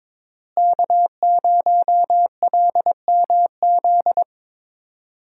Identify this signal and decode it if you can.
Morse code, heard as a keyed tone, about 22 wpm, 700 Hz. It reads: K0LMZ
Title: USA Call Signs - 5 Characters - Repeated 3x - 22wpm